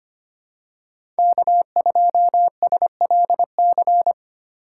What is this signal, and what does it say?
Morse code, heard as a keyed tone, about 25 wpm, 700 Hz. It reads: K2SLC
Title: USA Call Signs - 5 Characters - 3x Character Spacing - 25wpm